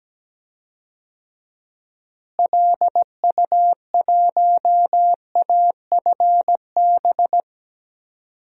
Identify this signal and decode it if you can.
Morse code, heard as a keyed tone, about 17 wpm, 700 Hz. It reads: LU1AFB
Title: DXCC-100 Call Signs - 6 Characters - Repeated 3x - 17wpm